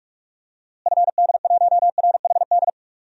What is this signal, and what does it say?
Morse code, heard as a keyed tone, about 47 wpm, 700 Hz. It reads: FD1RHD
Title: DXCC-100 Call Signs - 6 Characters - Repeated 2x - 45wpm